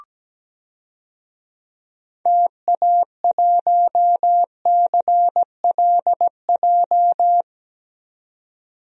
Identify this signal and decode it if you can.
Morse code, heard as a keyed tone, about 17 wpm, 700 Hz. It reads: TA1CLJ
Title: DXCC-100 Call Signs - 6 Characters - 17wpm